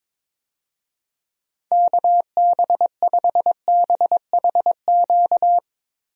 Morse code, heard as a keyed tone, about 22 wpm, 700 Hz. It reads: KB5BHQ
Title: USA Call Signs - 6 Characters - Repeated 3x - 22wpm